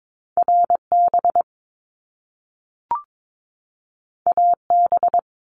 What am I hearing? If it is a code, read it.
Morse code, heard as a keyed tone, about 22 wpm, 700 Hz. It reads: RB AB
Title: Top Two Letter Combinations in Words 22wpm